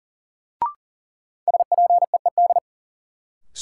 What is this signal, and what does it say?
Morse code, heard as a keyed tone, about 40 wpm, 700 Hz. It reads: SPEED
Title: Sentences from Top 800 Words - New Word Drill - 40wpm